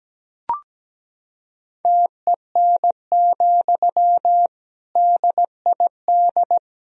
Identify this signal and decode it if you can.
Morse code, heard as a keyed tone, about 17 wpm, 700 Hz. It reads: TEN, DID
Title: Sentences from Top 400 Words and Abbreviations 17wpm